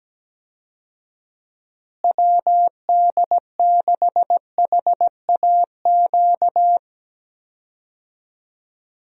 Morse code, heard as a keyed tone, about 17 wpm, 700 Hz. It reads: WD6HAQ
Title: USA Call Signs - 6 Characters - Repeated 2x - 17wpm